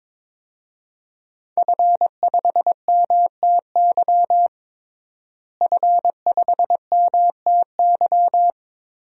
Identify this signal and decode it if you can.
Morse code, heard as a keyed tone, about 21 wpm, 700 Hz. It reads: F5MTY F5MTY
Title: DXCC-100 Call Signs - 5 Characters - Repeated 3x - 22wpm